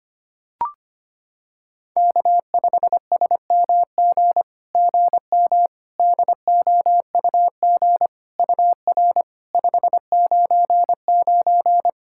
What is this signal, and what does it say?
Morse code, heard as a keyed tone, about 25 wpm, 690 Hz. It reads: K5SMG GM DOUG UR 599
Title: SOTA QSO Practice 25wpm